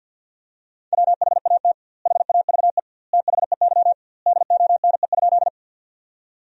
Morse code, heard as a keyed tone, about 52 wpm, 700 Hz. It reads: WHAT HAVE THEY DONE?
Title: 2x Word-Spacing - Sentences from Top 500 Words - 50wpm